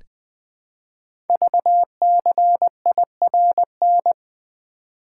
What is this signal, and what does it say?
Morse code, heard as a keyed tone, about 20 wpm, 700 Hz. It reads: VCIRN
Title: Code Groups - 5 Letters - 20wpm